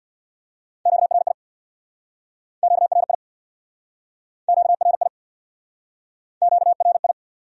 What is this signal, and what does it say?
Morse code, heard as a keyed tone, about 63 wpm, 700 Hz. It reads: QRI QRI QRI QRI